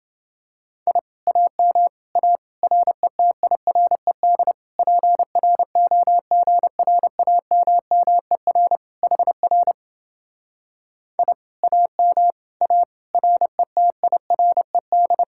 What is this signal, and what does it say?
Morse code, heard as a keyed tone, about 29 wpm, 700 Hz. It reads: I AM A RETIRED PROGRAMMER HR I AM A RETIRED